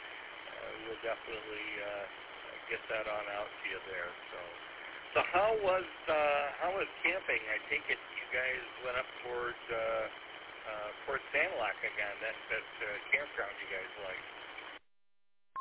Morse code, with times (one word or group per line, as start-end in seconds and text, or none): none